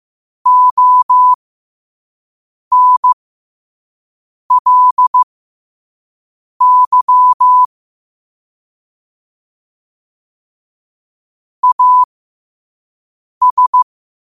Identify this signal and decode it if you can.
Morse code, heard as a keyed tone, about 15 wpm, spaced as for 5 wpm, 1000 Hz. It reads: ONLY AS
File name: qotdfiles / 5wpm